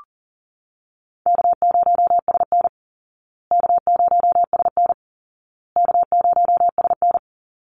Morse code, heard as a keyed tone, about 40 wpm, 700 Hz. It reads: K0SN K0SN K0SN
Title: Call Signs Repeated 2x - 40wpm